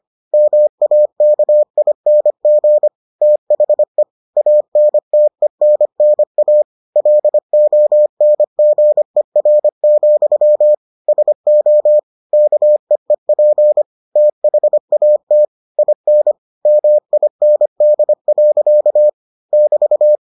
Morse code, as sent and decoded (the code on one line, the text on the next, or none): -- .- -.- .. -. --. / - .... / .- -. - . -. -. .- / .-.. --- -. --. .-. --..-- / ... --- / -.- . .--. / - .... .- - / .. -. / -- .. -. -.. .-.-.- / -...-
MAKING THE ANTENNA LONGER, SO KEEP THAT IN MIND. <BT>